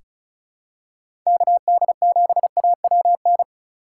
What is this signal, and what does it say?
Morse code, heard as a keyed tone, about 36 wpm, 700 Hz. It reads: KD7AWN
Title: USA Call Signs - 4x Character Spacing - 35wpm